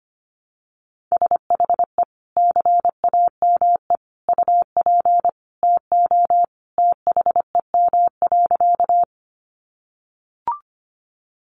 Morse code, heard as a keyed tone, about 25 wpm, 700 Hz. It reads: SHE CAME UP TO THEM.
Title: Sentences from Top 200 Words 25wpm